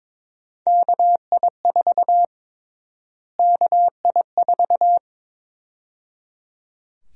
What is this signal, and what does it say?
Morse code, heard as a keyed tone, about 22 wpm, 700 Hz. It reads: KI4 KI4
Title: USA Call Sign Prefixes - Repeated 2x - 22wpm